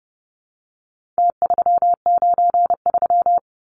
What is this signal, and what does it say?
Morse code, heard as a keyed tone, about 30 wpm, 700 Hz. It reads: T393
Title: Numbers - Contest Style - 4-Digit 30wpm